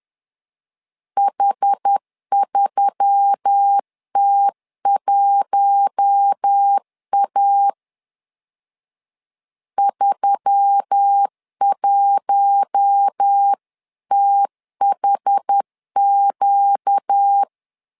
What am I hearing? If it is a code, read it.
Morse code, heard as a keyed tone, about 10 wpm, 800 Hz. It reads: H3T1A 31THQ
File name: LEZIONI / 5 LBVC / 5 L 10wpm 2259